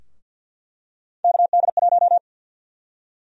Morse code, heard as a keyed tone, about 46 wpm, 700 Hz. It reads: KD1
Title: USA Call Sign Prefixes - Repeated 2x - 50wpm